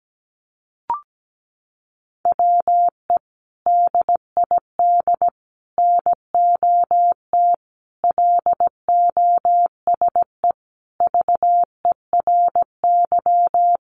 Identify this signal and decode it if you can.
Morse code, heard as a keyed tone, about 17 wpm, 700 Hz. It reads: WE DID NOT LOSE VERY
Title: Sentences from Top 1100 Words 17wpm